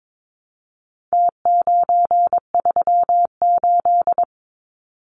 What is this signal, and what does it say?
Morse code, heard as a keyed tone, about 22 wpm, 700 Hz. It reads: T938
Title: Numbers - Contest Style - 4-Digit 22wpm